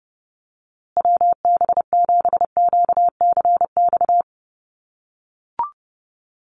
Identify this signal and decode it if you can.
Morse code, heard as a keyed tone, about 30 wpm, 700 Hz. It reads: WB7QCX
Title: USA Call Signs Repeated 2x 30wpm